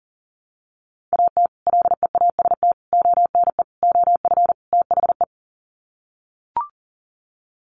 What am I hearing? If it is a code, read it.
Morse code, heard as a keyed tone, about 40 wpm, 700 Hz. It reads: AT LEAST ONE OF THE
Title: Sets of 5 Words Encore 40wpm